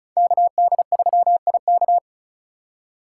Morse code, heard as a keyed tone, about 42 wpm, 700 Hz. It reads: KD3IK
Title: Speed Racing - USA 2x2 Call Signs 35wpm